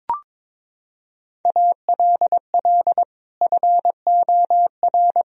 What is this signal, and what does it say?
Morse code, heard as a keyed tone, about 22 wpm, 700 Hz. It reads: ALL FOR
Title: Sentences from Top 1000 Words - Review Part 2 - 22wpm